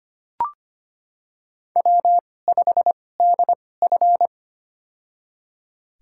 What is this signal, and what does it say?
Morse code, heard as a keyed tone, about 25 wpm, spaced as for 18 wpm, 700 Hz. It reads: W5DF
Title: USA Call Signs - 4 Characters - 2x Character Spacing - 25wpm